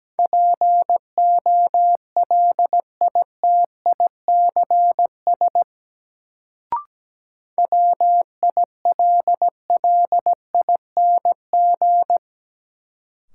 Morse code, heard as a keyed tone, about 17 wpm, 700 Hz. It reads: POLITICS WILLING
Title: Sentences from Top 1000 Words - New Word Drill - 17wpm